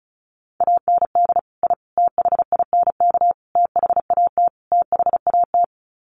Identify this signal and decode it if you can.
Morse code, heard as a keyed tone, about 34 wpm, 700 Hz. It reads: AND I THINK THAT THAT
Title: Sets of 5 Words Encore 35wpm